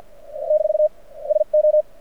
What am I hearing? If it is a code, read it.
Morse code, heard as a keyed tone, about 48 wpm, spaced as for 40 wpm, 600 Hz. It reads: - SO
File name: archived / CRSnet